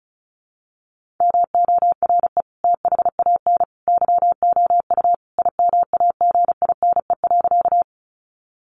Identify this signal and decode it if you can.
Morse code, heard as a keyed tone, about 34 wpm, 700 Hz. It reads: MORE THAN YOU IMAGINE.
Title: Sentences from Top 1400 Words 35wpm